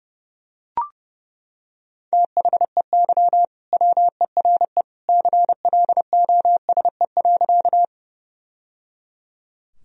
Morse code, heard as a keyed tone, about 30 wpm, 700 Hz. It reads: THEY WERE CLOSE.
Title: Sentences from Top 1000 Words - Review Part 1 - 30wpm